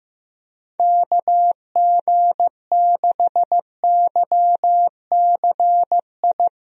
Morse code, heard as a keyed tone, about 15 wpm, 700 Hz. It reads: KG6YCI
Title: USA Call Signs - 6 Characters - Repeated 3x - 15wpm